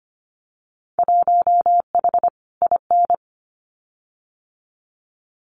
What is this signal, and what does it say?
Morse code, heard as a keyed tone, about 25 wpm, 700 Hz. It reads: 1H IN
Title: Winter Field Day Exchanges 25wpm